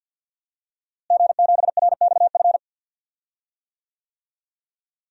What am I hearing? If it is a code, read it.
Morse code, heard as a keyed tone, about 47 wpm, 700 Hz. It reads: G7RXF
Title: DXCC-100 Call Signs - 5 Characters - Repeated 2x - 50wpm